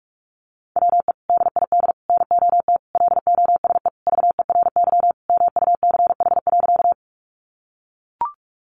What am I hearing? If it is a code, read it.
Morse code, heard as a keyed tone, about 44 wpm, 700 Hz. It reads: WE DID NOT LOSE VERY MUCH.
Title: Sentences from Top 1100 Words 45wpm